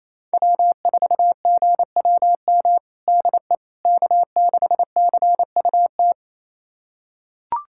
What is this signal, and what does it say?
Morse code, heard as a keyed tone, about 27 wpm, 700 Hz. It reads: W4GWM DE K6CUT
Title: USA Callsign DE Callsign - 5 Characters - 28wpm